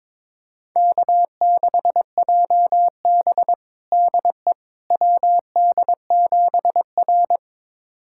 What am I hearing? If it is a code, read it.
Morse code, heard as a keyed tone, about 22 wpm, 700 Hz. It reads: K6JB DE WD7R
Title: USA Callsign DE Callsign - 4 Characters - 22wpm